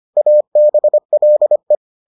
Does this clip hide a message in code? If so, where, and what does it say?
Yes: Morse code from 0.2 to 1.8 s, heard as a keyed tone, about 25 wpm, 600 Hz. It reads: ABLE